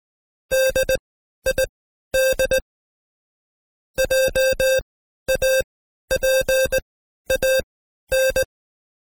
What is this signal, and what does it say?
Morse code, heard as a keyed tone, about 19 wpm, spaced as for 12 wpm, 530 Hz. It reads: DID JAPAN
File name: Code Quick / AudioDownld2 / Seg132-Stepper5 11WPM